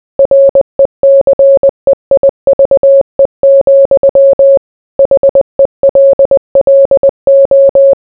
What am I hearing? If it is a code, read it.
Morse code, heard as a keyed tone, about 20 wpm, 550 Hz. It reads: RECEIVE, HELLO